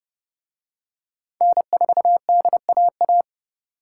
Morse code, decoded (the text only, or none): N4DAA